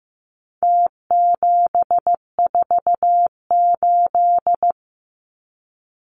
Morse code, heard as a keyed tone, about 15 wpm, 700 Hz. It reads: T748
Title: Numbers - Contest Style - 4-Digit 15wpm